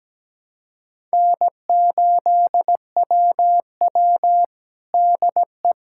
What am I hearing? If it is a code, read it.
Morse code, heard as a keyed tone, about 17 wpm, 700 Hz. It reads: N8WW DE
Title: USA Callsign DE Callsign - 4 Characters - 17wpm